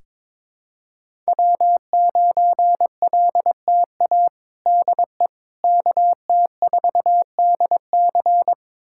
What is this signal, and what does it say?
Morse code, heard as a keyed tone, about 22 wpm, 700 Hz. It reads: W9LTA DE KT4DC
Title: USA Callsign DE Callsign - 5 Characters - 22wpm